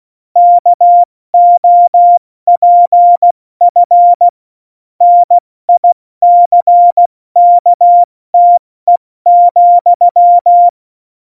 Morse code, heard as a keyed tone, about 16 wpm, spaced as for 14 wpm, 700 Hz. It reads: KOPF NICKTE,
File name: Morsekurs / Witze / Witz 21 Juli 2020, 70 F80